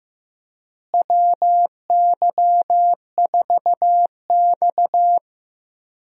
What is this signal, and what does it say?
Morse code, heard as a keyed tone, about 15 wpm, 700 Hz. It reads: WY4X
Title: USA Call Signs - 4 Characters - Repeated 2x - 15wpm